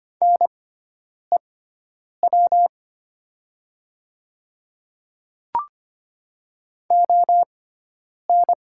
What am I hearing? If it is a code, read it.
Morse code, heard as a keyed tone, about 25 wpm, spaced as for 9 wpm, 690 Hz. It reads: NEW ON
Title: 6x Character Spacing - Top 200 Words - 25wpm